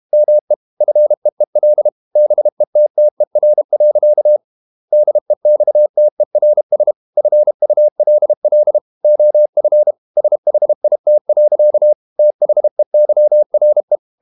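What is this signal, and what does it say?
Morse code, heard as a keyed tone, about 32 wpm, 600 Hz. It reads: ME FEEL BETTER. DEXTERS FULL OF SHIT. THEYRE
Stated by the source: Morse code practice at 32 wpm from CW med Gnister, reading Ken Follett's novel Never in English.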